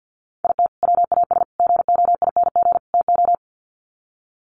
Morse code, heard as a keyed tone, about 50 wpm, 700 Hz. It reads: IT WAS GOING TO
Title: Sets of 4 Words 50wpm